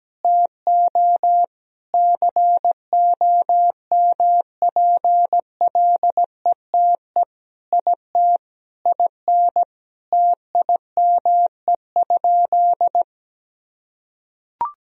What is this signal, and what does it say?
Morse code, heard as a keyed tone, about 17 wpm, 700 Hz. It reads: TO COMPLETE IT IN TIME?
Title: Sentences from Top 1000 Words - Review Part 3 - 17wpm